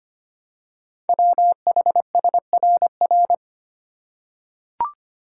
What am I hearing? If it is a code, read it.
Morse code, heard as a keyed tone, about 25 wpm, 700 Hz. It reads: WHSRR
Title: Code Groups - 5 Letters - 25wpm